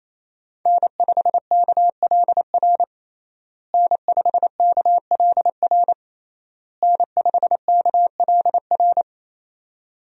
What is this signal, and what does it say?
Morse code, heard as a keyed tone, about 28 wpm, 700 Hz. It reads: N5KLR N5KLR N5KLR
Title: USA Call Signs - 5 Characters - Repeated 3x - 28wpm